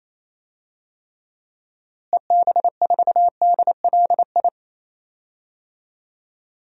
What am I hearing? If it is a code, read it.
Morse code, heard as a keyed tone, about 28 wpm, 700 Hz. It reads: EB4DLI
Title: DXCC-100 Call Signs - 6 Characters - Repeated 2x - 28wpm